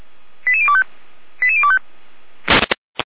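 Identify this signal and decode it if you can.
Five-tone selective call (ZVEI), heard as quick runs of five tones, tones of about 70 ms. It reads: 88025 88025
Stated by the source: ZVEI five-tone selective call received on SDR, 455 channel path.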